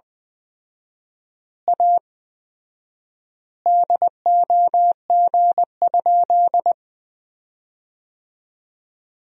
Morse code, heard as a keyed tone, about 20 wpm, 700 Hz. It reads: A DOG?
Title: 4x Word-Spacing - Sentences from Top 400 Words - 20wpm